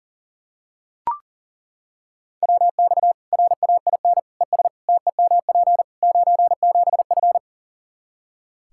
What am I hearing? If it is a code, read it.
Morse code, heard as a keyed tone, about 40 wpm, 700 Hz. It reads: WX RAIN ES TEMP 97F
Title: Practice QSO - Exchange 2 - Info - 40wpm